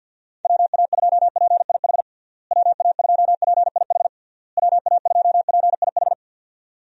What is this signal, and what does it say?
Morse code, heard as a keyed tone, about 51 wpm, 700 Hz. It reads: WA2PIH WA2PIH WA2PIH
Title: USA Call Signs - 6 Characters - Repeated 3x - 50wpm